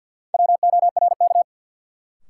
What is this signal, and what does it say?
Morse code, heard as a keyed tone, about 52 wpm, 700 Hz. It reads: WORK
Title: Sentences from Top 200 Words - New Word Drill - 50wpm